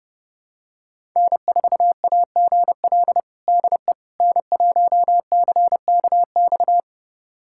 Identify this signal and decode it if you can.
Morse code, heard as a keyed tone, about 30 wpm, 700 Hz. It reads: N4AGL DE N1CKX
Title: USA Callsign DE Callsign - 5 Characters - 30wpm